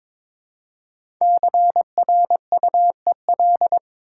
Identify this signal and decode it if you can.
Morse code, heard as a keyed tone, about 22 wpm, 700 Hz. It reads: CRUEL
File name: 5 Letter Words - First 3 Letters Given - 22wpm